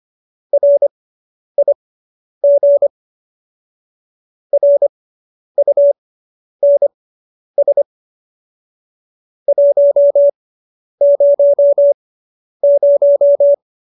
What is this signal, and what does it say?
Morse code, heard as a keyed tone, about 25 wpm, spaced as for 10 wpm, 570 Hz. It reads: RIG RUNS 100